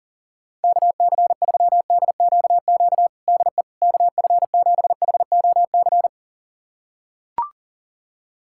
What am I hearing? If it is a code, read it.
Morse code, heard as a keyed tone, about 40 wpm, 700 Hz. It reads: KC3DQQ DE KF7HOC
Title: USA Callsign DE Callsign 40wpm